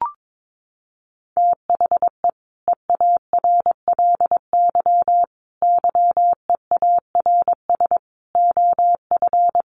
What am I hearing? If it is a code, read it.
Morse code, heard as a keyed tone, about 22 wpm, 700 Hz. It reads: THE EARLY YEARS OF
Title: Sets of 5 Words Encore 22wpm